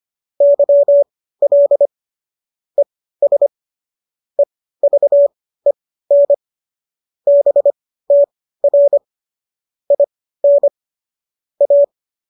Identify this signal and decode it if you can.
Morse code, heard as a keyed tone, about 25 wpm, spaced as for 15 wpm, 570 Hz. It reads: YL ES EVEN BTR IN A